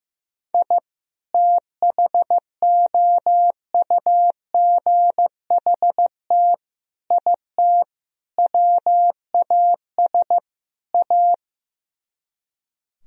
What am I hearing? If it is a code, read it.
Morse code, heard as a keyed tone, about 15 wpm, 700 Hz. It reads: I THOUGHT IT WAS A
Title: Speed-Racing - Sets of 5 Words 15wpm